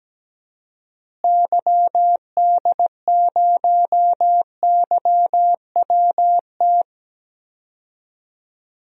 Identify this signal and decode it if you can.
Morse code, heard as a keyed tone, about 17 wpm, 700 Hz. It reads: YD0YWT
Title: DXCC-100 Call Signs - 6 Characters - Repeated 3x - 17wpm